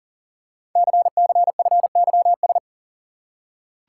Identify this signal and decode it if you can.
Morse code, heard as a keyed tone, about 39 wpm, 700 Hz. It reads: CCFYS
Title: Code Groups - 5 Letters - 40wpm